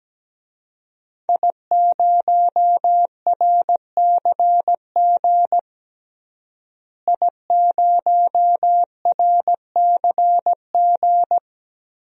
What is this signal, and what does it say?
Morse code, heard as a keyed tone, about 17 wpm, 700 Hz. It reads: I0RCG I0RCG
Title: DXCC-100 Call Signs - 5 Characters - Repeated 2x - 17wpm